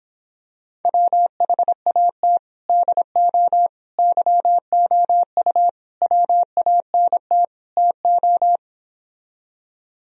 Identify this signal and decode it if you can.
Morse code, heard as a keyed tone, about 26 wpm, 700 Hz. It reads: WHAT DO YOU WANT TO